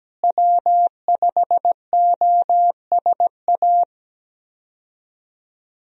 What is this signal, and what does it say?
Morse code, heard as a keyed tone, about 17 wpm, 700 Hz. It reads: W5OSA
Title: USA Call Signs - 5 Characters - Repeated 2x - 17wpm